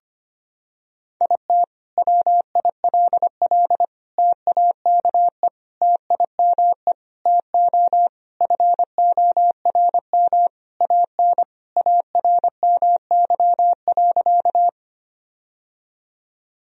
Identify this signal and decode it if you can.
Morse code, heard as a keyed tone, about 25 wpm, 700 Hz. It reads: IT WILL TAKE TIME TO FORM AN ARMY.
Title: Sentences from Top 1000 Words - Review Part 1 - 25wpm